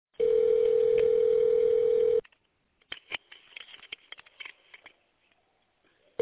Ringback tone (North America), 440 + 480 Hz.